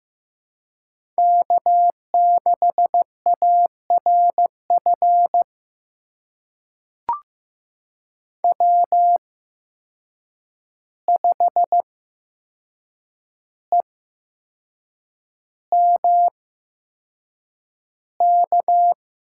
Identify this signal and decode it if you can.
Morse code, heard as a keyed tone, about 15 wpm, spaced as for 6 wpm, 700 Hz. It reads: K6ARF W5EMK
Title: USA Call Signs - 5 Characters - 8x Character Spacing - 15wpm